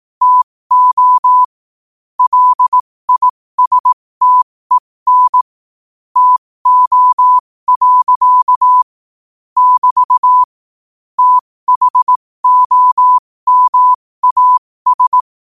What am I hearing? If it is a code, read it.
Morse code, heard as a keyed tone, about 18 wpm, spaced as for 15 wpm, 1000 Hz. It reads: TO LISTEN TO. <BT> THOMAS
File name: qotdfiles / 15wpm